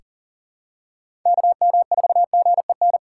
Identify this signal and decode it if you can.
Morse code, heard as a keyed tone, about 40 wpm, 700 Hz. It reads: KM4GEN
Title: USA Call Signs - 3x Character Spacing - 40wpm